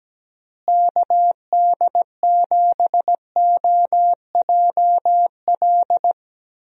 Morse code, heard as a keyed tone, about 17 wpm, 700 Hz. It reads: KD7OJL
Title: USA Call Signs - 6 Characters - Repeated 2x - 17wpm